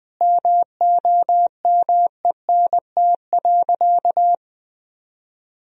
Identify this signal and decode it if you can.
Morse code, heard as a keyed tone, about 20 wpm, 700 Hz. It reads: MOMENT.
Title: Sentences from Top 1000 Words - Review Part 1 - 20wpm